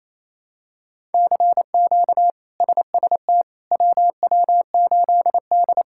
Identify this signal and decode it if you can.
Morse code, heard as a keyed tone, about 28 wpm, 700 Hz. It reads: CQ SST WW8D
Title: K1USN SST QSO Practice - 28wpm